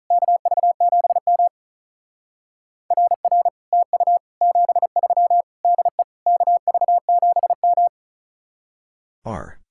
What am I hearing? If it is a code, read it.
Morse code, heard as a keyed tone, about 43 wpm, 700 Hz. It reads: KV7M RR TU 73 DE KV7M
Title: Speed-Racing Common QSO Phrases - 35wpm